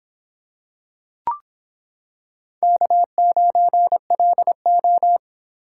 Morse code, heard as a keyed tone, about 26 wpm, 700 Hz. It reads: K9LO